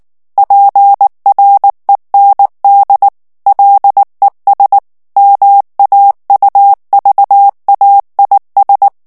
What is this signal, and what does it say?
Morse code, heard as a keyed tone, about 19 wpm, 800 Hz. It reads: PREND LES MAUVAIS
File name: morse mp3 / FRANCAIS TEXTE CLAIR 1 VITESSE 18 MOTS MINUTES